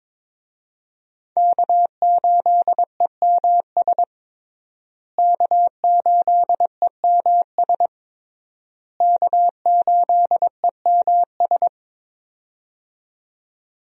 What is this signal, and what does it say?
Morse code, heard as a keyed tone, about 22 wpm, 690 Hz. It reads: K8EMS K8EMS K8EMS
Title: USA Call Signs - 5 Characters - Repeated 3x - 22wpm